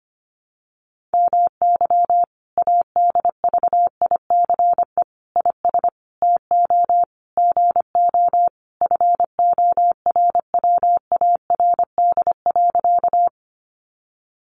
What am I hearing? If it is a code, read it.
Morse code, heard as a keyed tone, about 25 wpm, 700 Hz. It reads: MY ADVICE IS TO GO FORWARD.
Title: Sentences from Top 1300 Words 25wpm